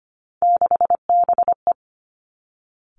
Morse code, heard as a keyed tone, about 25 wpm, 700 Hz. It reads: 6BE
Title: Three Challenging Letter-Number Combinations 25wpm